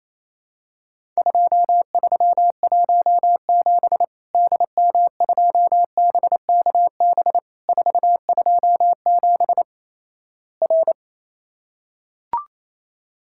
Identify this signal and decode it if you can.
Morse code, heard as a keyed tone, about 28 wpm, 700 Hz. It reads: 2317 DM2BKB 427 R
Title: Worked All Europe DX Contest - QTC Exchange - 28wpm